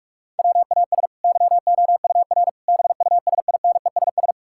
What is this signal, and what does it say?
Morse code, heard as a keyed tone, about 42 wpm, 700 Hz. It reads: WAS YOUR BUSINESS
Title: Sentences from Top 1000 Words - Review Part 5 - 45wpm